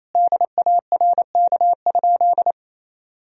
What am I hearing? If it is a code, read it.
Morse code, heard as a keyed tone, about 28 wpm, 700 Hz. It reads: DARK?
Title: Sentences from Top 500 Words and Abbreviations 28wpm